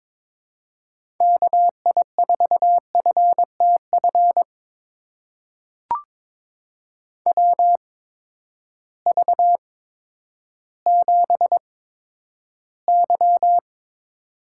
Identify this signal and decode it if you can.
Morse code, heard as a keyed tone, about 22 wpm, spaced as for 11 wpm, 700 Hz. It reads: KI4FTF WV7Y
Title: USA Call Signs - 8x Character Spacing - 22wpm